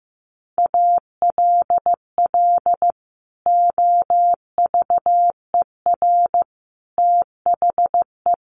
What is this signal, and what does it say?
Morse code, heard as a keyed tone, about 15 wpm, 700 Hz. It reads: ALL OVER THE
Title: Sets of 4 Words 15wpm